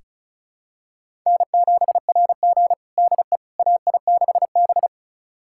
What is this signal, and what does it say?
Morse code, heard as a keyed tone, about 36 wpm, 700 Hz. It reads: N7RG DE AI6B
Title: USA Callsign DE Callsign - 4 Characters - 35wpm